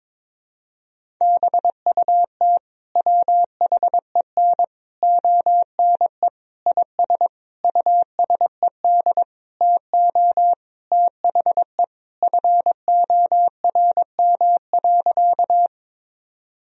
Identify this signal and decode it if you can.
Morse code, heard as a keyed tone, about 22 wpm, 700 Hz. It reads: BUT WHEN ONE IS USED TO THE FORM.
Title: Sentences from Top 1000 Words - Review Part 1 - 22wpm